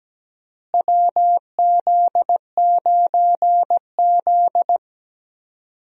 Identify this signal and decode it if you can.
Morse code, heard as a keyed tone, about 17 wpm, 700 Hz. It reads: WZ9Z
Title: USA Call Signs - 4 Characters - Repeated 3x - 17wpm